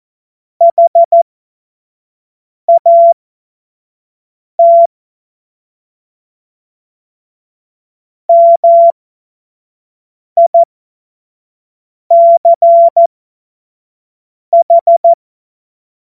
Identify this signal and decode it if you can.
Morse code, heard as a keyed tone, about 14 wpm, spaced as for 5 wpm, 680 Hz. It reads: HAT MICH
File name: Morsekurs / Witze / Witz 22. Februar 2022 25 F70